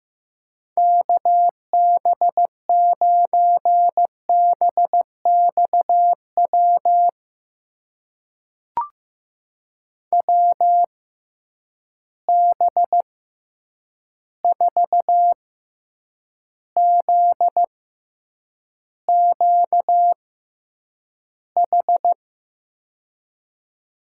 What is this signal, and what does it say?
Morse code, heard as a keyed tone, about 15 wpm, spaced as for 8 wpm, 700 Hz. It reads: KB9BXW WB4ZQH
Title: USA Call Signs - 6 Characters - 6x Character Spacing - 15wpm